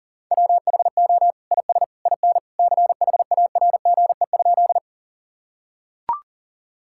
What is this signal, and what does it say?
Morse code, heard as a keyed tone, about 40 wpm, 700 Hz. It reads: WHO IS IN CHARGE?
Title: Sentences from Top 1000 Words - Review Part 6 - 40wpm